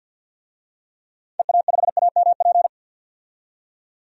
Morse code, heard as a keyed tone, about 47 wpm, 700 Hz. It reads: EA5AMP
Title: DXCC-100 Call Signs - 6 Characters - 50wpm